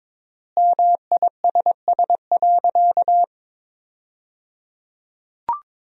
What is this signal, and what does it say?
Morse code, heard as a keyed tone, about 22 wpm, 700 Hz. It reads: MISS.
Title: 3x Word-Spacing - Sentences from Top 500 Words - 22wpm